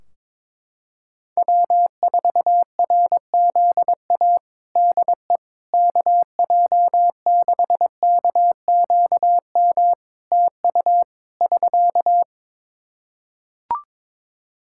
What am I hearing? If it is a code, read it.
Morse code, heard as a keyed tone, about 21 wpm, 700 Hz. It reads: W4RZA DE KJ6KQM TU <SK>
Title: Practice QSO - Exchange 4 - Ending - 22wpm